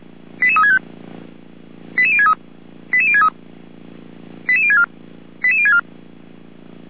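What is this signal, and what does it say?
Five-tone selective call (ZVEI), heard as quick runs of five tones, tones of about 70 ms. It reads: 88365 88063 88063 88064 88064
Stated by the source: Received on SDR, ZVEI five-tone call 88352.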